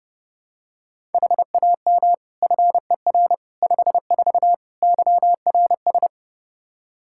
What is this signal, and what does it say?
Morse code, heard as a keyed tone, about 30 wpm, 700 Hz. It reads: HAM FER 54 YRS
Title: Practice QSO - Exchange 3 - More Info - 30wpm